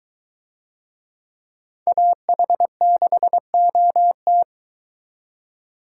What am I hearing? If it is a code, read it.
Morse code, heard as a keyed tone, about 21 wpm, 700 Hz. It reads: AH6OT